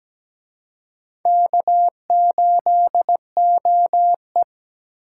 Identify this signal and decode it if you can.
Morse code, heard as a keyed tone, about 20 wpm, 700 Hz. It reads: K8OE
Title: Speed Racing - USA Call Signs 17wpm